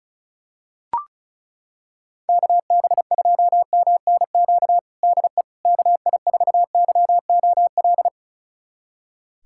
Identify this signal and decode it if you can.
Morse code, heard as a keyed tone, about 36 wpm, 700 Hz. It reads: KB2MNQ DE KI4YOL
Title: USA Callsign DE Callsign - 6 Characters - 35wpm